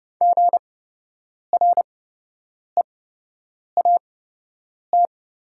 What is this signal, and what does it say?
Morse code, heard as a keyed tone, about 30 wpm, spaced as for 8 wpm, 700 Hz. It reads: GREAT
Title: 8x Character Spacing - Top 300 Words - 30wpm